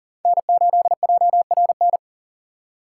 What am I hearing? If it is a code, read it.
Morse code, heard as a keyed tone, about 40 wpm, 700 Hz. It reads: N8JRN
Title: USA Call Signs - 5 Characters - 2x Character Spacing - 40wpm